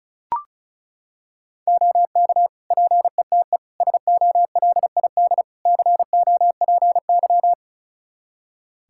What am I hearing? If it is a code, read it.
Morse code, heard as a keyed tone, about 36 wpm, 700 Hz. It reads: OK PETE SOLID COPY
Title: Practice QSO - Exchange 3 - More Info - 35wpm